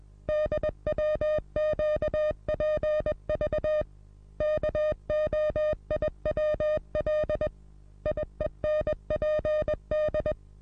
Morse code, heard as a keyed tone, about 21 wpm, 610 Hz. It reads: DWQPV KOIWL IENPD
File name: repos / Morsekurs / DJ4UF Morsen26